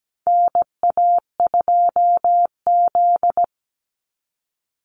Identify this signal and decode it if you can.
Morse code, heard as a keyed tone, about 17 wpm, 700 Hz. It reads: NA2Z
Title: USA Call Signs 17wpm